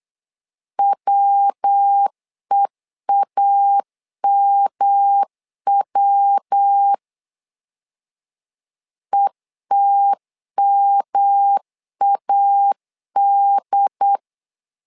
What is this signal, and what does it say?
Morse code, heard as a keyed tone, about 8 wpm, 800 Hz. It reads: WEAMW ETMAD
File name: LEZIONI / 3 MWRO / 3 MW 8wpm 2501